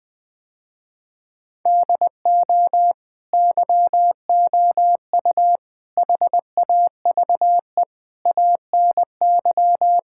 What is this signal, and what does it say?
Morse code, heard as a keyed tone, about 20 wpm, 700 Hz. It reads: DO YOU HAVE ANY
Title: Sentences with 4-Word Phrases - Spoken Beginning - 20wpm